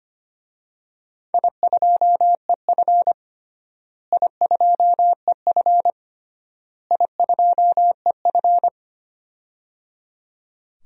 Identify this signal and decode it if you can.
Morse code, heard as a keyed tone, about 25 wpm, 700 Hz. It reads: I2EF I2EF I2EF
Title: DXCC-100 Call Signs - 4 Characters - Repeated 3x - 25wpm